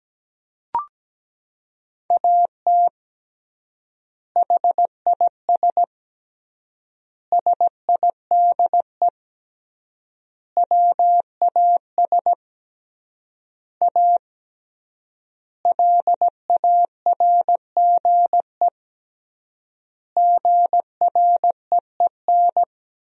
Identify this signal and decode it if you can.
Morse code, heard as a keyed tone, about 17 wpm, 700 Hz. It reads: AT HIS SIDE WAS A LARGE GREEN
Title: 3x Word-Spacing - Sentences from Top 500 Words - 17wpm